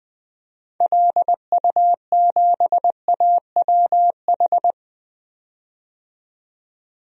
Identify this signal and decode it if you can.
Morse code, heard as a keyed tone, about 20 wpm, 700 Hz. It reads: LU7AWH
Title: DXCC-100 Call Signs - 6 Characters - 20wpm